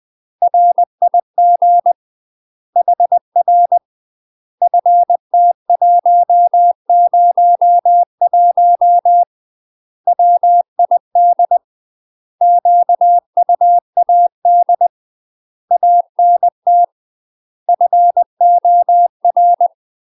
Morse code, heard as a keyed tone, about 20 wpm, 700 Hz. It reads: RIG HR FT101 WID QUAD ANT FOR
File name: UK MASTER QSOs TRAINING / 20WPM / UK MASTER QSOs 0037